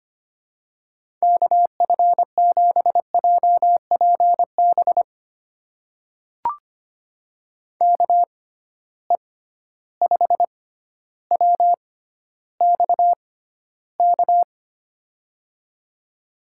Morse code, heard as a keyed tone, about 25 wpm, spaced as for 13 wpm, 700 Hz. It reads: KF7JPB KE5WXK
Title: USA Call Signs - 6 Characters - 6x Character Spacing - 25wpm